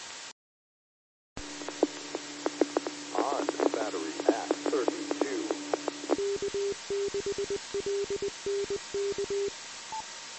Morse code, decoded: none

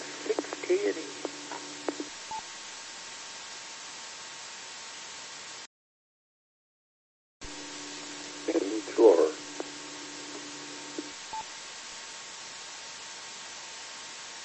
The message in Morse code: none

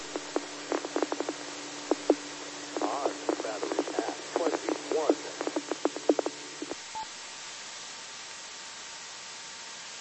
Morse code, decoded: none

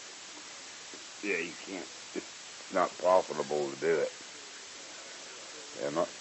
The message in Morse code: none

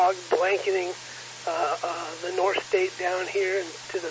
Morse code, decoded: none

